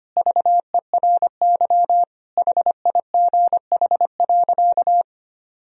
Morse code, heard as a keyed tone, about 25 wpm, 700 Hz. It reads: VERY HIGH.
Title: Sentences from Top 1000 Words - Review Part 1 - 25wpm